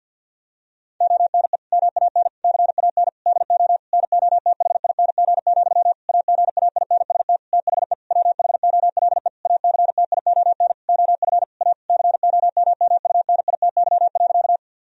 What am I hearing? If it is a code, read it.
Morse code, heard as a keyed tone, about 50 wpm, 700 Hz. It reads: ONE MAN CAN DO NOTHING, AGAINST THE WHOLE ACTION OF A COMMUNITY.
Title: Sentences from Top 1000 Words - Review Part 4 - 50wpm